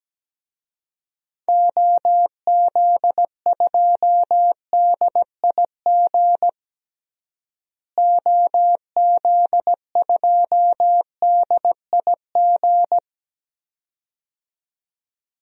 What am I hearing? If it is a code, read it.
Morse code, heard as a keyed tone, about 17 wpm, 700 Hz. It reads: OZ2DIG OZ2DIG
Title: DXCC-100 Call Signs - 6 Characters - Repeated 2x - 17wpm